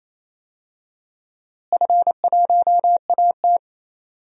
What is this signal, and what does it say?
Morse code, heard as a keyed tone, about 28 wpm, 700 Hz. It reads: F1AT
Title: DXCC-100 Call Signs - 4 Characters - Repeated 3x - 28wpm